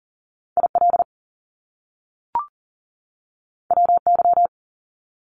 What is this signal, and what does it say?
Morse code, heard as a keyed tone, about 39 wpm, 700 Hz. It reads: IL WY
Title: US States 40wpm